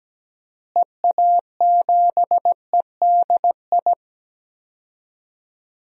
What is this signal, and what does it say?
Morse code, heard as a keyed tone, about 17 wpm, 700 Hz. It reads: EA7EDI
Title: DXCC-100 Call Signs - 6 Characters - Repeated 2x - 17wpm